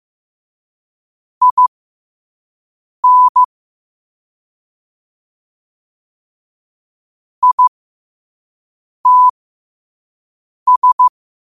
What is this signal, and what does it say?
Morse code, heard as a keyed tone, about 15 wpm, spaced as for 5 wpm, 1000 Hz. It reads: IN ITS
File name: qotdfiles / 5wpm